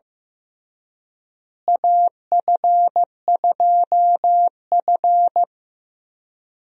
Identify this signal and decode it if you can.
Morse code, heard as a keyed tone, about 15 wpm, 700 Hz. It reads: AF2F
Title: USA Call Signs - 4 Characters - Repeated 3x - 15wpm